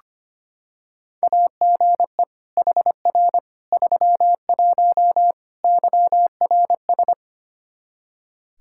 Morse code, heard as a keyed tone, about 25 wpm, 700 Hz. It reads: AGE HR 31 YRS
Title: Practice QSO - Exchange 3 - More Info - 25wpm